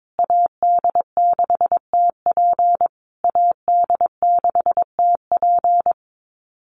Morse code, heard as a keyed tone, about 22 wpm, 700 Hz. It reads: AD6TP AD6TP
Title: USA Call Signs Repeated 3x 22wpm